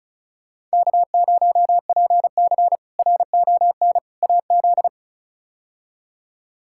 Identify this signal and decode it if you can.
Morse code, heard as a keyed tone, about 36 wpm, 700 Hz. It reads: K0PC RON AZ
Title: K1USN SST QSO Practice - 35wpm